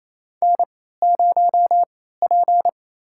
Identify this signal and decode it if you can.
Morse code, heard as a keyed tone, about 27 wpm, spaced as for 16 wpm, 700 Hz. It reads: N0P
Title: USA Call Signs - 3x Character Spacing - 28wpm